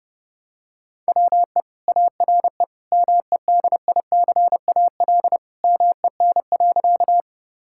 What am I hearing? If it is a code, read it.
Morse code, heard as a keyed tone, about 30 wpm, 700 Hz. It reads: WE ARE MEDICAL MEN.
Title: Sentences from Top 1000 Words - Review Part 5 - 30wpm